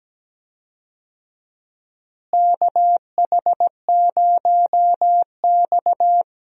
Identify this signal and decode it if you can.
Morse code, heard as a keyed tone, about 17 wpm, 700 Hz. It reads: KH0X
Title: USA Call Signs - 4 Characters - Repeated 2x - 17wpm